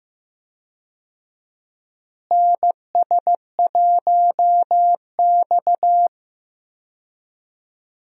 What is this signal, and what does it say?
Morse code, heard as a keyed tone, about 15 wpm, 700 Hz. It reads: NS1X
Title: USA Call Signs - 4 Characters - Repeated 2x - 15wpm